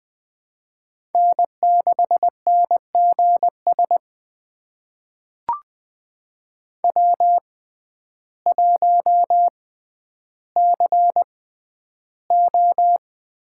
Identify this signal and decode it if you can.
Morse code, heard as a keyed tone, about 20 wpm, spaced as for 11 wpm, 700 Hz. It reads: N6NGS W1CO
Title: USA Call Signs - 5 Characters - 6x Character Spacing - 20wpm